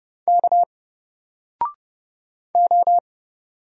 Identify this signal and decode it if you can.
Morse code, heard as a keyed tone, about 30 wpm, 700 Hz. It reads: K O
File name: Single-Letter-Number-30wpm